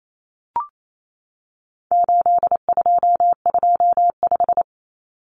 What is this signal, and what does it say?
Morse code, heard as a keyed tone, about 28 wpm, 700 Hz. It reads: 8225
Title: Common Cut Numbers - 4-Digit 28wpm